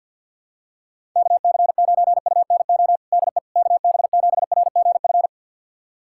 Morse code, heard as a keyed tone, about 46 wpm, 700 Hz. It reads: KC9UNO DE KB7RGF
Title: USA Callsign DE Callsign 50wpm